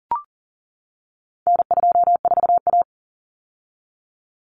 Morse code, heard as a keyed tone, about 40 wpm, 700 Hz. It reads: N24A
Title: Common Cut Numbers - 4-Digit 40wpm